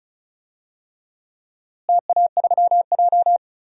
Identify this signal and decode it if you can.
Morse code, heard as a keyed tone, about 36 wpm, 700 Hz. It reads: TA3J
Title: DXCC-100 Call Signs - 4 Characters - Repeated 3x - 35wpm